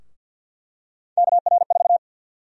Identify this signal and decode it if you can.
Morse code, heard as a keyed tone, about 48 wpm, 700 Hz. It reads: KR4
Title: USA Call Sign Prefixes - Repeated 2x - 50wpm